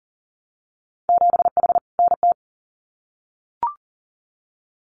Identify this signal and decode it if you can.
Morse code, heard as a keyed tone, about 40 wpm, 700 Hz. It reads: 7H NT
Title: Winter Field Day Exchanges Encore 40wpm